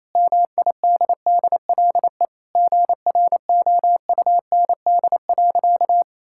Morse code, heard as a keyed tone, about 28 wpm, 700 Hz. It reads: MIDDLE GROUND.
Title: Sentences from Top 1600 Words 28wpm